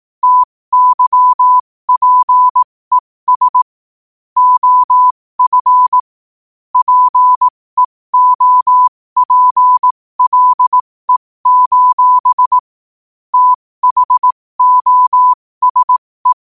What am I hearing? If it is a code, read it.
Morse code, heard as a keyed tone, about 18 wpm, spaced as for 15 wpm, 1000 Hz. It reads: TYPES OF PEOPLE: THOSE
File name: qotdfiles / 15wpm